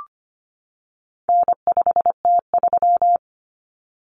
Morse code, heard as a keyed tone, about 25 wpm, 700 Hz. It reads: N5T3
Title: Common Cut Numbers - 4-Digit 25wpm